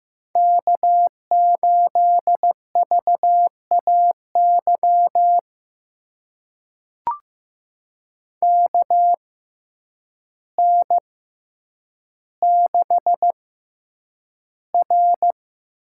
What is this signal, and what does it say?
Morse code, heard as a keyed tone, about 15 wpm, spaced as for 8 wpm, 700 Hz. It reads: K8VAY KN6R
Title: USA Call Signs - 6x Character Spacing - 15wpm